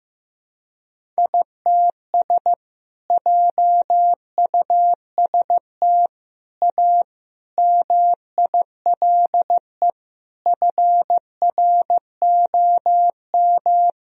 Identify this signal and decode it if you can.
Morse code, heard as a keyed tone, about 16 wpm, 700 Hz. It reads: ITS JUST A MILE FROM
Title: Sentences from Top 1000 Words - Review Part 1 - 15wpm